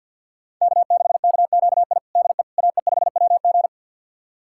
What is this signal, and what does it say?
Morse code, heard as a keyed tone, about 52 wpm, 700 Hz. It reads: K6KQI DE AE5WG
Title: USA Callsign DE Callsign - 5 Characters - 50wpm